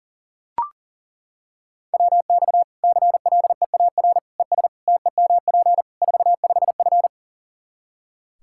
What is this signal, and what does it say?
Morse code, heard as a keyed tone, about 40 wpm, 700 Hz. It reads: WX CLEAR ES TEMP 45F
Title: Practice QSO - Exchange 2 - Info - 40wpm